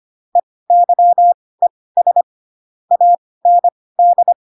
Morse code, heard as a keyed tone, about 25 wpm, spaced as for 18 wpm, 700 Hz. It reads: EYES AND